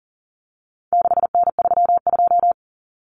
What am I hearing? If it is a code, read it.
Morse code, heard as a keyed tone, about 40 wpm, 700 Hz. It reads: 6N32
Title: Common Cut Numbers - 4-Digit 40wpm